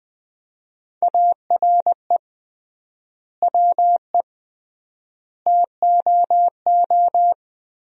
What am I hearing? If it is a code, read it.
Morse code, heard as a keyed tone, about 21 wpm, 700 Hz. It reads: ARE WE TOO
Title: 3x Word-Spacing - Sentences from Top 300 Words - 20wpm